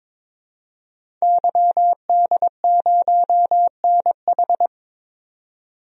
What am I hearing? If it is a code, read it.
Morse code, heard as a keyed tone, about 22 wpm, 700 Hz. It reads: YD0NH
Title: DXCC-100 Call Signs - 5 Characters - 22wpm